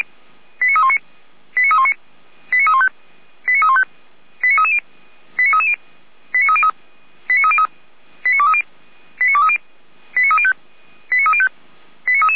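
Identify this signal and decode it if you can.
Five-tone selective call (ZVEI), heard as quick runs of five tones, tones of about 80 ms. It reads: 89319 89319 89315 89315 89330 89330 89303 89303 89230 89230 89305 89305
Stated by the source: Received via SDR; ZVEI five-tone selective call 89125.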